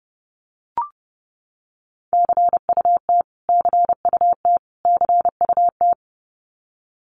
Morse code, heard as a keyed tone, about 30 wpm, 700 Hz. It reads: CUT CUT CUT
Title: Top 500 Words Repeated 30wpm